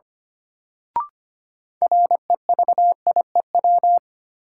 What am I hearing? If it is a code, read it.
Morse code, heard as a keyed tone, about 25 wpm, 700 Hz. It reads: REVIEW
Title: Sentences from Top 900 Words - New Word Drill - 25wpm